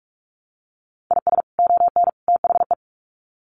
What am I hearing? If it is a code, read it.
Morse code, heard as a keyed tone, about 44 wpm, 700 Hz. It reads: IS ON THE
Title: Sets of 3 Words 45wpm